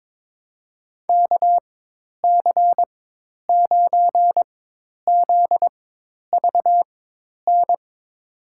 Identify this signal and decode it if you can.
Morse code, heard as a keyed tone, about 22 wpm, spaced as for 10 wpm, 700 Hz. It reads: KC9ZVN
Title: USA Call Signs - 6 Characters - 4x Character Spacing - 22wpm